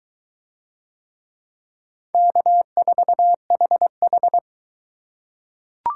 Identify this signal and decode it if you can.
Morse code, heard as a keyed tone, about 23 wpm, 700 Hz. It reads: K4HH